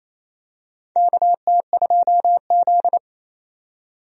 Morse code, heard as a keyed tone, about 28 wpm, 700 Hz. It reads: KT2Z
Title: USA Call Signs - 4 Characters - 28wpm